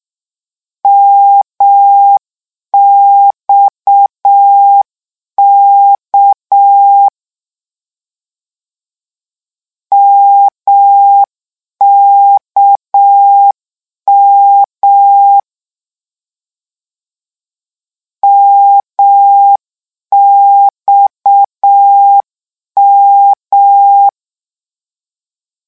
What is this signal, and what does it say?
Morse code, heard as a keyed tone, about 6 wpm, 800 Hz. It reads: MXK MKM MXM
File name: morse mp3 / lecon 1-vitesse 6